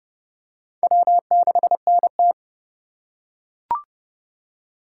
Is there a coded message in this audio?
Yes: Morse code from 0.8 to 2.3 s, heard as a keyed tone, about 28 wpm, 700 Hz. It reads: W6NT